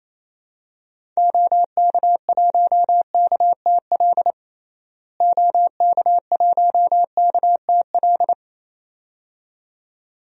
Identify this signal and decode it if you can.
Morse code, heard as a keyed tone, about 27 wpm, 700 Hz. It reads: OK1KTL OK1KTL
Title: DXCC-100 Call Signs - 6 Characters - Repeated 2x - 28wpm